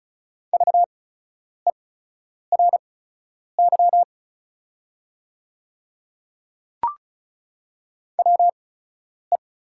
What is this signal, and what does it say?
Morse code, heard as a keyed tone, about 36 wpm, spaced as for 10 wpm, 700 Hz. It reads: VERY WE
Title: 8x Character Spacing - Top 300 Words - 35wpm